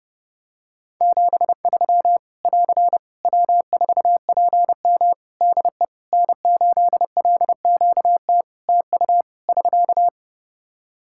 Morse code, heard as a keyed tone, about 31 wpm, 700 Hz. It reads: 73 <AR> W4PM DE N8LQT TU <SK>
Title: Practice QSO - Exchange 4 - Ending - 30wpm